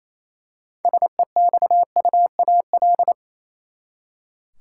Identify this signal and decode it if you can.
Morse code, heard as a keyed tone, about 28 wpm, 700 Hz. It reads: SEXUAL
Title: Six Letter Words - 28wpm